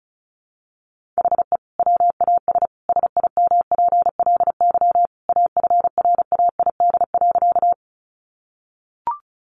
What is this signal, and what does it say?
Morse code, heard as a keyed tone, about 34 wpm, 700 Hz. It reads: HE WAS SIMPLY AFRAID.
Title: Sentences from Top 1500 Words 35wpm